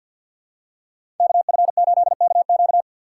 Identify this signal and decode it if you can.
Morse code, heard as a keyed tone, about 53 wpm, 700 Hz. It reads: KF8KQ
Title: USA Call Signs - 5 Characters - 8x Character Spacing - 50wpm